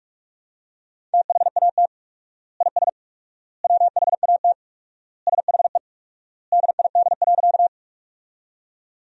Morse code, heard as a keyed tone, about 46 wpm, 700 Hz. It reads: THAT IS WHAT SHE DID.
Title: 4x Word-Spacing - Sentences from Top 100 Words - 45wpm